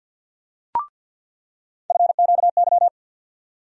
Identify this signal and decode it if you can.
Morse code, heard as a keyed tone, about 52 wpm, 700 Hz. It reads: FQY
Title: USA Call Sign Postfixes - 3 Letters 50wpm